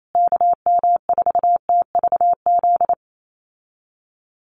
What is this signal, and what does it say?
Morse code, heard as a keyed tone, about 28 wpm, 700 Hz. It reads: KM4TVZ
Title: USA Call Signs Repeated 3x 28wpm